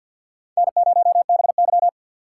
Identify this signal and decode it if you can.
Morse code, heard as a keyed tone, about 49 wpm, 700 Hz. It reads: N0BY
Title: USA Call Signs - 4 Characters - 3x Character Spacing - 50wpm